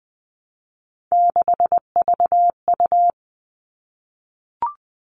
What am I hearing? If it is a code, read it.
Morse code, heard as a keyed tone, about 20 wpm, 700 Hz. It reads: 6VU
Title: All Cut Numbers - 3-Digit 20wpm